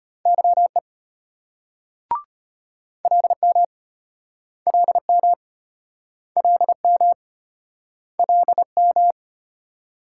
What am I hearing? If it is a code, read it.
Morse code, heard as a keyed tone, about 34 wpm, 700 Hz. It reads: YE LM LM LM LM